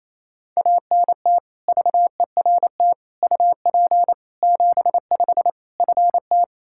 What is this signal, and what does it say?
Morse code, heard as a keyed tone, about 28 wpm, 700 Hz. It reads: ANT VERT UP 75 FT
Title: K1USN SST QSO Practice 28wpm